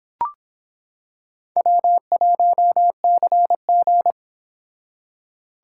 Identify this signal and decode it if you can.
Morse code, heard as a keyed tone, about 25 wpm, 700 Hz. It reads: W1CG